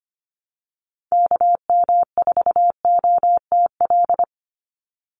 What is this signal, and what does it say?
Morse code, heard as a keyed tone, about 25 wpm, 700 Hz. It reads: KM4OTL
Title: USA Call Signs 25wpm